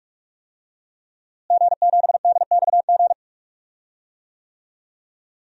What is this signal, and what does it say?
Morse code, heard as a keyed tone, about 46 wpm, 700 Hz. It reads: G7DXG
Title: DXCC-100 Call Signs - 5 Characters - Repeated 3x - 45wpm